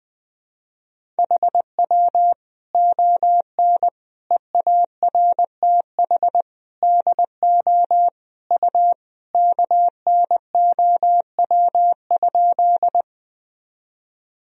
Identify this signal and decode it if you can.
Morse code, heard as a keyed tone, about 20 wpm, 700 Hz. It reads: HW ON EARTH DO U KNOW?
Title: Sentences from Top 200 Words and Abbreviations 20wpm